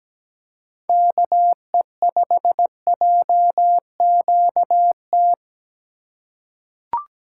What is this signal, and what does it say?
Morse code, heard as a keyed tone, about 17 wpm, 700 Hz. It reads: KE5JQT
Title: USA Call Signs - 6 Characters - 3x Character Spacing - 17wpm